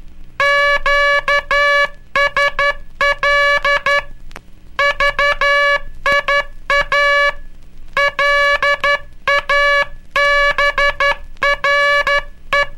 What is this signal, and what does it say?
Morse code, heard as a keyed tone, about 12 wpm, 1700 Hz. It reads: QSL VIA LABRE
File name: cursos / Curso CW mp3 / CW Aula - 18